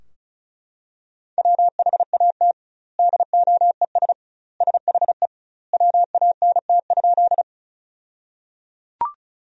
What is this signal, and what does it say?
Morse code, heard as a keyed tone, about 36 wpm, 700 Hz. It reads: WHAT DOES SHE WANT?
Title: 2x Word-Spacing - Sentences from Top 200 Words - 35wpm